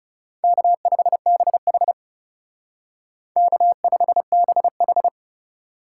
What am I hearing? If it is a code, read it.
Morse code, heard as a keyed tone, about 39 wpm, 700 Hz. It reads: K5BH K5BH